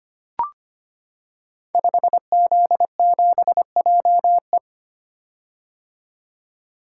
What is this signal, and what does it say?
Morse code, heard as a keyed tone, about 25 wpm, 700 Hz. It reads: 5Z7JE
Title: Code Groups - 5 Letter:Numbers - 25wpm